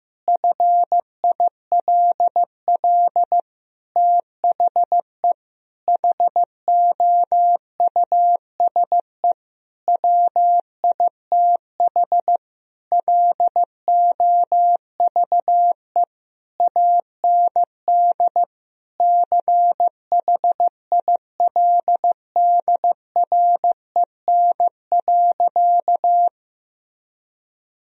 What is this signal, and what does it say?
Morse code, heard as a keyed tone, about 15 wpm, 700 Hz. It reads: FILL THE HOUSE WITH LOVE AND CHILDREN.
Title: Sentences from Top 1000 Words - Review Part 3 - 15wpm